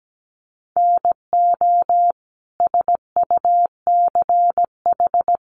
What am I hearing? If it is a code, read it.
Morse code, heard as a keyed tone, about 17 wpm, 700 Hz. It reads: NO SUCH
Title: Sentences from Top 1500 Words 17wpm